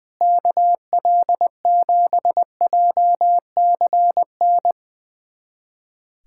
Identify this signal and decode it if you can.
Morse code, heard as a keyed tone, about 20 wpm, 700 Hz. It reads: KL7JCN
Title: USA Callsign DE Callsign 20wpm